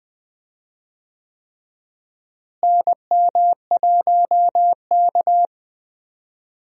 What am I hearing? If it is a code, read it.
Morse code, heard as a keyed tone, about 20 wpm, 700 Hz. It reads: NM1K
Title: USA Call Signs - 4 Characters - Repeated 3x - 20wpm